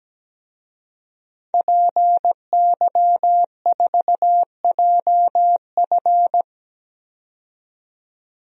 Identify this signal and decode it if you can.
Morse code, heard as a keyed tone, about 17 wpm, 700 Hz. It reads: PY4JF
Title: DXCC-100 Call Signs - 5 Characters - 17wpm